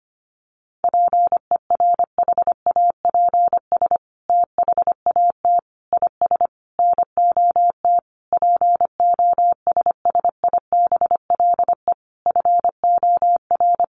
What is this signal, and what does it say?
Morse code, heard as a keyed tone, about 25 wpm, 700 Hz. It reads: PERHAPS THAT IS NOT POSSIBLE FOR
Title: Sentences from Top 900 Words 25wpm